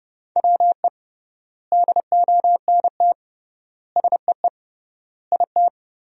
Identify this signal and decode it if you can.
Morse code, heard as a keyed tone, about 32 wpm, 700 Hz. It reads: WE DONT SEE IT
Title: 3x Word-Spacing - Sentences from Top 300 Words - 30wpm